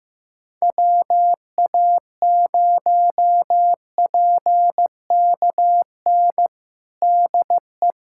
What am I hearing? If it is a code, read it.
Morse code, heard as a keyed tone, about 15 wpm, 700 Hz. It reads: WA0PKN DE
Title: USA Callsign DE Callsign - 6 Characters - 15wpm